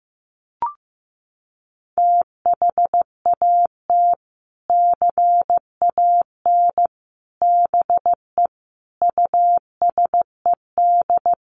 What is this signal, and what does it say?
Morse code, heard as a keyed tone, about 14 wpm, 700 Hz. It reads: THAT CAN BE USED
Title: Sets of 5 Words Encore 15wpm